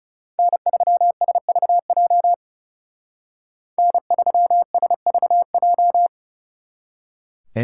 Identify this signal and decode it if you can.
Morse code, heard as a keyed tone, about 39 wpm, 700 Hz. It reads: N3SVJ N3SVJ